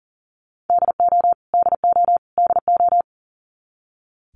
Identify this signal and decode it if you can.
Morse code, heard as a keyed tone, about 40 wpm, 700 Hz. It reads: DO DO DO
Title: Top 300 Words Repeated 40wpm